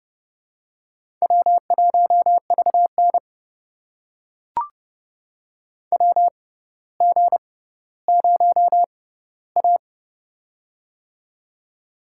Morse code, heard as a keyed tone, about 30 wpm, spaced as for 15 wpm, 700 Hz. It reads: W1VN WG0A
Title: USA Call Signs - 4 Characters - 6x Character Spacing - 30wpm